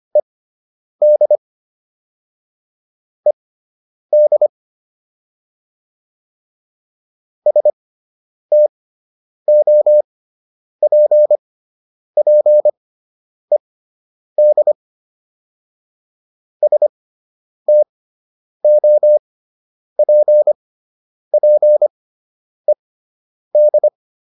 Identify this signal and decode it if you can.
Morse code, heard as a keyed tone, about 25 wpm, spaced as for 9 wpm, 600 Hz. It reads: ED ED STOPPED STOPPED